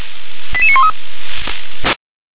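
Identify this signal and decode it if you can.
Five-tone selective call (ZVEI), heard as quick runs of five tones, tones of about 80 ms. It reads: 88913
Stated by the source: ZVEI five-tone call received by SDR.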